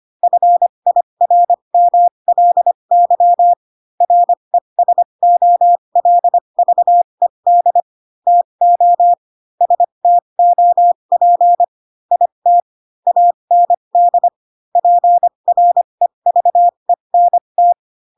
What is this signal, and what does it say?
Morse code, heard as a keyed tone, about 25 wpm, spaced as for 22 wpm, 700 Hz. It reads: FIRMLY RESOLVED TO STOP IT AND PREVENT
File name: Pinocchio / Pinocchio ch3 18 25 ik1hge